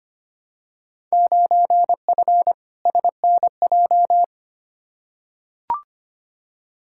Morse code, heard as a keyed tone, about 25 wpm, 700 Hz. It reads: 9F SNJ
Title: Field Day Exchanges Encore - 25wpm